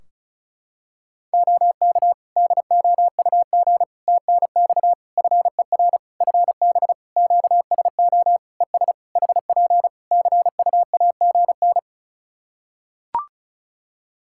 Morse code, heard as a keyed tone, about 36 wpm, 700 Hz. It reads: OK DOUG TNX FER FB QSO ES HP CUAGN
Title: Practice QSO - Exchange 4 - Ending - 35wpm